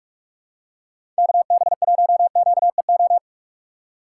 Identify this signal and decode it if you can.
Morse code, heard as a keyed tone, about 46 wpm, 700 Hz. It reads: KB1QEO
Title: USA Call Signs - 6 Characters - 4x Character Spacing - 45wpm